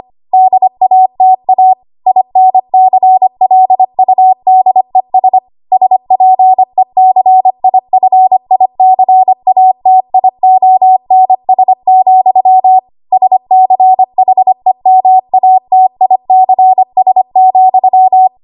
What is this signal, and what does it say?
Morse code, heard as a keyed tone, about 25 wpm, 750 Hz. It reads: DATA INCLUDES SPECIFICATIONS, SCHEMATICS,
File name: Archive / 25 WPM / 25WPM